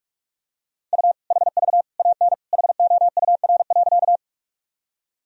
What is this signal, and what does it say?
Morse code, heard as a keyed tone, about 42 wpm, 700 Hz. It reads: U HV AN HOUR.
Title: Sentences from Top 400 Words and Abbreviations 45wpm